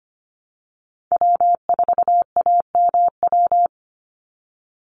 Morse code, heard as a keyed tone, about 25 wpm, 700 Hz. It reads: W4AMW
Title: NA SOTA Chaser Callsigns 25wpm